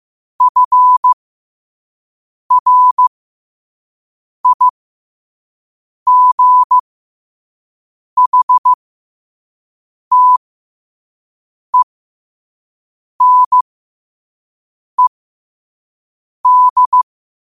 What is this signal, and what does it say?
Morse code, heard as a keyed tone, about 15 wpm, spaced as for 5 wpm, 1000 Hz. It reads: FRIGHTENED
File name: qotdfiles / 5wpm